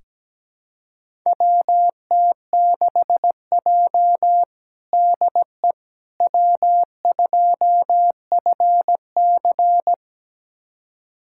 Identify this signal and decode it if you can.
Morse code, heard as a keyed tone, about 17 wpm, 700 Hz. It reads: WT6J DE W2FC
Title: USA Callsign DE Callsign - 4 Characters - 17wpm